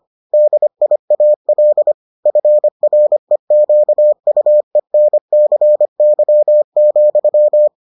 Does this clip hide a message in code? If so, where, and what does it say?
Yes: Morse code from 0.3 to 7.7 s, heard as a keyed tone, about 25 wpm, 600 Hz. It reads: DIAL FREQUENCY,